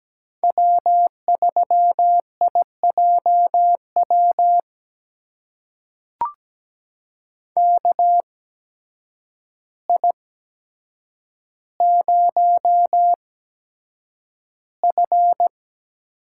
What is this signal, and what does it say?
Morse code, heard as a keyed tone, about 17 wpm, spaced as for 8 wpm, 700 Hz. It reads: W3IJW KI0F
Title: USA Call Signs - 5 Characters - 8x Character Spacing - 17wpm